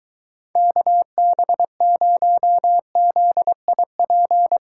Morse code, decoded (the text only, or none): KB0ZIP